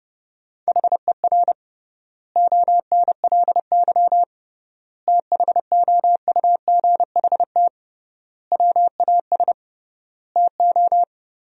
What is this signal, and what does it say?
Morse code, heard as a keyed tone, about 29 wpm, 700 Hz. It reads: HER ONLY THOUGHT WAS TO
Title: 3x Word-Spacing - Sentences from Top 400 Words - 30wpm